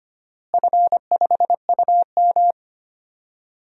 Morse code, heard as a keyed tone, about 25 wpm, 700 Hz. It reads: F5UM
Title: DXCC-100 Call Signs - 4 Characters - Repeated 3x - 25wpm